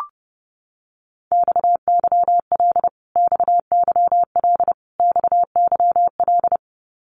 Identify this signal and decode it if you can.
Morse code, heard as a keyed tone, about 30 wpm, 700 Hz. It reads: XYL XYL XYL
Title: Abbreviations Repeated 3x - 30wpm